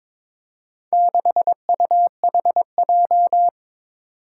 Morse code, heard as a keyed tone, about 23 wpm, 700 Hz. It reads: 6UHJ
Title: Four Challenging Letter-Number Combinations 22wpm